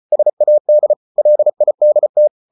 Morse code, heard as a keyed tone, about 34 wpm, 600 Hz. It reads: SAD LIDT
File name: Ferie QRQ 34wpm